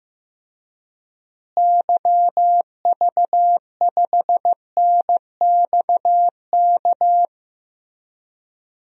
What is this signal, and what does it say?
Morse code, heard as a keyed tone, about 15 wpm, 700 Hz. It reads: YV5NXK
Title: DXCC-100 Call Signs - 6 Characters - Repeated 2x - 15wpm